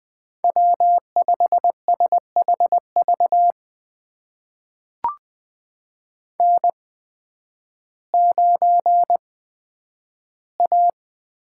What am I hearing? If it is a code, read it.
Morse code, heard as a keyed tone, about 20 wpm, spaced as for 11 wpm, 700 Hz. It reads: W5SHV N9A
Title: USA Call Signs - 5 Characters - 8x Character Spacing - 20wpm